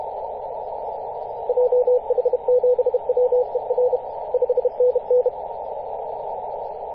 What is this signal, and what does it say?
Morse code, heard as a keyed tone, about 30 wpm, 500 Hz, with radio noise. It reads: JH7WER 5NN
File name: xr0ys-21-007mhz-cw